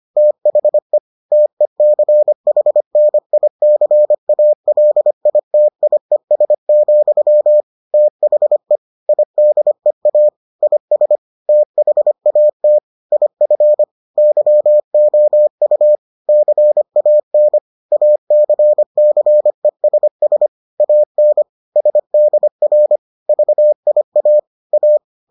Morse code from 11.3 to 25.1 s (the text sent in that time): THAT IF YOU CAN ACCESS AN SDR VIA A